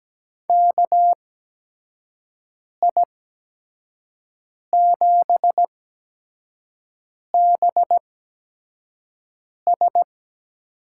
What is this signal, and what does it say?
Morse code, heard as a keyed tone, about 17 wpm, spaced as for 5 wpm, 700 Hz. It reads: KI7BS
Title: USA Call Signs - 8x Character Spacing - 17wpm